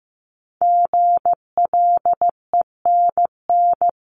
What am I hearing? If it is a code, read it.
Morse code, heard as a keyed tone, about 15 wpm, 700 Hz. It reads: GLENN
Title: CWT Exchange 15wpm